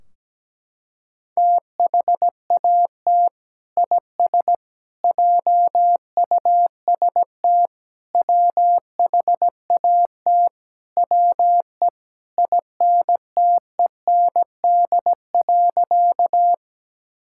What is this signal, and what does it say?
Morse code, heard as a keyed tone, about 17 wpm, 700 Hz. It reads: THAT IS JUST WHAT WE INTEND.
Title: Sentences from Top 1600 Words 17wpm